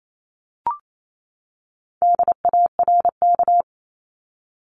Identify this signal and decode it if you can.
Morse code, heard as a keyed tone, about 28 wpm, 700 Hz. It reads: DARK
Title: Top 500 Words 28wpm